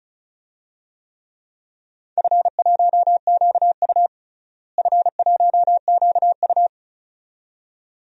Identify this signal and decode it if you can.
Morse code, heard as a keyed tone, about 36 wpm, 700 Hz. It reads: F1QU F1QU
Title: DXCC-100 Call Signs - 4 Characters - Repeated 2x - 35wpm